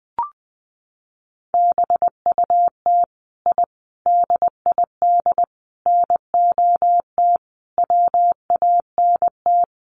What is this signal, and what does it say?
Morse code, heard as a keyed tone, about 20 wpm, 700 Hz. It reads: BUT I DID NOT WANT
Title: Sets of 5 Words Encore 20wpm